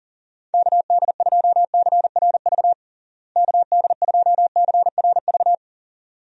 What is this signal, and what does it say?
Morse code, heard as a keyed tone, about 40 wpm, 700 Hz. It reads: KD2CRV KD2CRV
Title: USA Call Signs - 6 Characters - Repeated 2x - 40wpm